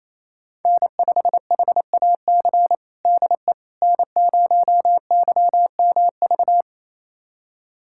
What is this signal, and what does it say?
Morse code, heard as a keyed tone, about 28 wpm, 700 Hz. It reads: N5HAC DE N0YMV
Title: USA Callsign DE Callsign - 5 Characters - 28wpm